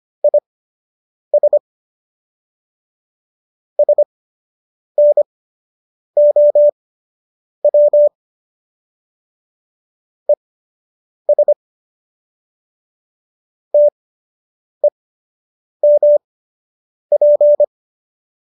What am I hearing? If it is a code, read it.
Morse code, heard as a keyed tone, about 25 wpm, spaced as for 8 wpm, 590 Hz. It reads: IS SNOW ES TEMP